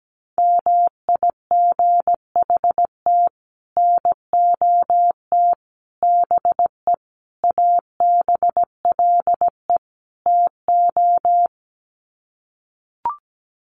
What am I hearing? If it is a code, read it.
Morse code, heard as a keyed tone, about 17 wpm, 700 Hz. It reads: MIGHT NOT BE ABLE TO
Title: Sets of 5 Words Encore2 17wpm